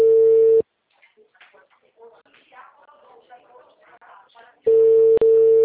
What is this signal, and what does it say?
Ringback tone (China, 450 Hz), with speech.